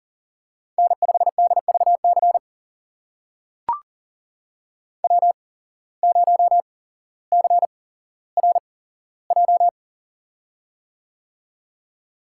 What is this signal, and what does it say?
Morse code, heard as a keyed tone, about 40 wpm, spaced as for 17 wpm, 700 Hz. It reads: N5DVC W0CRJ
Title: USA Call Signs - 5 Characters - 8x Character Spacing - 40wpm